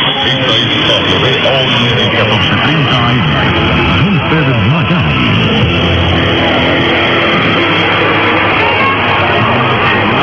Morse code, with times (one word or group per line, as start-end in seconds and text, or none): none